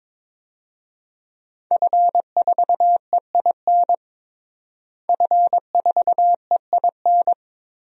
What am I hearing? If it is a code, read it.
Morse code, heard as a keyed tone, about 22 wpm, 700 Hz. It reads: F4EIN F4EIN
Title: DXCC-100 Call Signs - 5 Characters - Repeated 2x - 22wpm